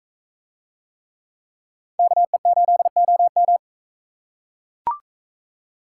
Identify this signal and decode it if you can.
Morse code, heard as a keyed tone, about 42 wpm, 700 Hz. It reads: KE8OM